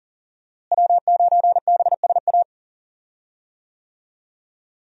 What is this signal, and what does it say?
Morse code, heard as a keyed tone, about 38 wpm, 700 Hz. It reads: W9BSA
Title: USA Call Signs - 5 Characters - Repeated 2x - 40wpm